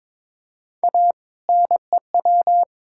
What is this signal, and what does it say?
Morse code, heard as a keyed tone, about 22 wpm, 700 Hz. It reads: A NEW
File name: Build A Phrase - 2-Word Phrases - 22wpm